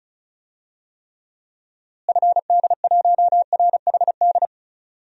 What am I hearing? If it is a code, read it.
Morse code, heard as a keyed tone, about 34 wpm, 700 Hz. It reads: FD1RHD
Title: DXCC-100 Call Signs - 6 Characters - Repeated 2x - 35wpm